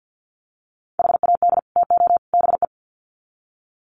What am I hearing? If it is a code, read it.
Morse code, heard as a keyed tone, about 50 wpm, 700 Hz. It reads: HAD TO BE
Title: Sets of 3 Words 50wpm